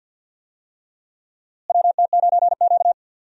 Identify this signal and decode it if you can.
Morse code, heard as a keyed tone, about 52 wpm, 700 Hz. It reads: WT9Q
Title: USA Call Signs - 4 Characters - Repeated 3x - 50wpm